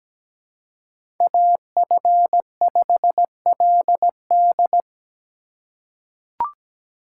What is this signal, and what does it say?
Morse code, heard as a keyed tone, about 17 wpm, 700 Hz. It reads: AF5LD
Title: USA Call Signs - 5 Characters - 3x Character Spacing - 17wpm